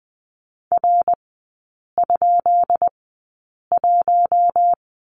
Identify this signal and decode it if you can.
Morse code, heard as a keyed tone, about 20 wpm, spaced as for 8 wpm, 690 Hz. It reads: R?1
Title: 133 Lesson 22e - Question Mark - Three Characters 20wpm